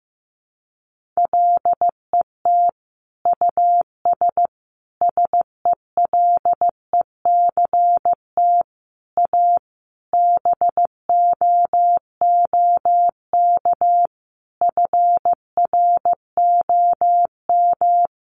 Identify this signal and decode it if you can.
Morse code, heard as a keyed tone, about 15 wpm, 700 Hz. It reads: LET US SELECT A BOOK FROM
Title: Sentences from Top 1300 Words 15wpm